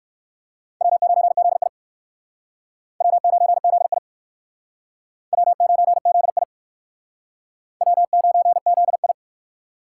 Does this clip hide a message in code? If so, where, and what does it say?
Yes: Morse code from 0.8 to 9.1 s, heard as a keyed tone, about 68 wpm, 700 Hz. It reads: W9ZI W9ZI W9ZI W9ZI